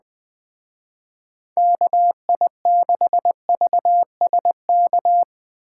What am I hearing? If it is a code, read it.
Morse code, heard as a keyed tone, about 20 wpm, 700 Hz. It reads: KI6VSK
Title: Speed Racing - USA 2x3 Call Signs 17wpm